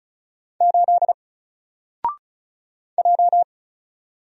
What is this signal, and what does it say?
Morse code, heard as a keyed tone, about 36 wpm, 700 Hz. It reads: 8 J
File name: Single-Letter-Number-35wpm